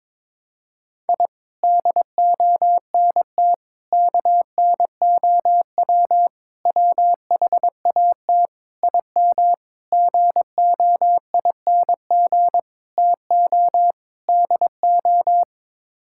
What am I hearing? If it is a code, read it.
Morse code, heard as a keyed tone, about 22 wpm, 700 Hz. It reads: I DONT KNOW WHAT IM GOING TO DO
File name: practice / 22wpm / Sets of 8 Words - 22wpm